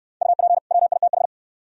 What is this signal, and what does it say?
Morse code, heard as a keyed tone, about 69 wpm, 700 Hz. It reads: UP WITH